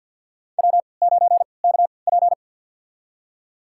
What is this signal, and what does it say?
Morse code, heard as a keyed tone, about 51 wpm, spaced as for 28 wpm, 700 Hz. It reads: W9KP
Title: USA Call Signs - 4 Characters - 3x Character Spacing - 50wpm